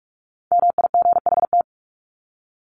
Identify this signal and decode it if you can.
Morse code, heard as a keyed tone, about 44 wpm, 700 Hz. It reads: MIGHT
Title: Top 300 Words 45wpm